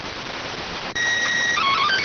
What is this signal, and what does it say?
SSTV (slow-scan television), PD 290.